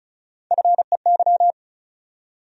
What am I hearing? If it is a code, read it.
Morse code, heard as a keyed tone, about 36 wpm, 700 Hz. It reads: FEY
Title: USA Call Sign Postfixes - 3 Letters 35wpm